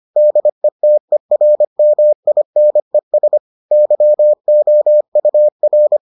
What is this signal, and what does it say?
Morse code, heard as a keyed tone, about 25 wpm, 600 Hz. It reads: DETERMINES YOUR